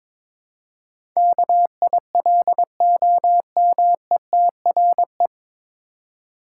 Morse code, heard as a keyed tone, about 22 wpm, 700 Hz. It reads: KILOMETRE
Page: Morse Code Ninja practice 22wpm